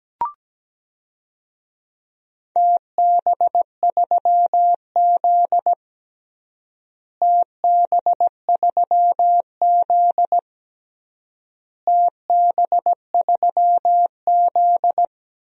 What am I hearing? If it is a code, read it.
Morse code, heard as a keyed tone, about 17 wpm, 700 Hz. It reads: TB3Z TB3Z TB3Z
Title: DXCC-100 Call Signs - 4 Characters - Repeated 3x - 17wpm